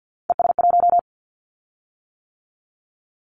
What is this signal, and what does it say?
Morse code, heard as a keyed tone, about 50 wpm, 700 Hz. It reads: ES1
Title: Callsign Prefixes for Top 50 DXCC - 50wpm